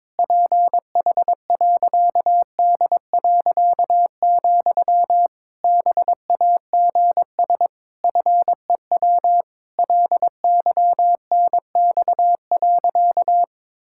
Morse code, heard as a keyed tone, about 22 wpm, 700 Hz. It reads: PH.D., BAGS FEW LYNX.
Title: Pangram Challenge v2 - 22wpm